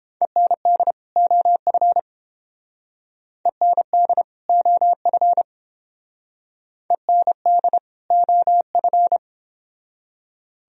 Morse code, heard as a keyed tone, about 30 wpm, 700 Hz. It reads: END OF END OF END OF